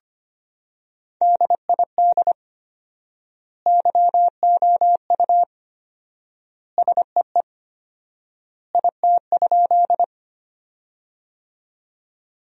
Morse code, heard as a keyed tone, about 25 wpm, 700 Hz. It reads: DID YOU SEE IT?
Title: 4x Word-Spacing - Sentences from Top 100 Words - 25wpm